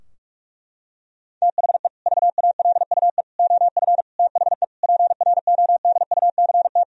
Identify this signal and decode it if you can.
Morse code, heard as a keyed tone, about 46 wpm, 700 Hz. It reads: THE VALUE OF THE PRODUCT
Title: Sentences from Top 1000 Words - Review Part 3 - 45wpm